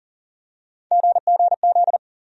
Morse code, heard as a keyed tone, about 38 wpm, 700 Hz. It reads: GGZ
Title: USA Call Sign Postfixes - 3 Letters 40wpm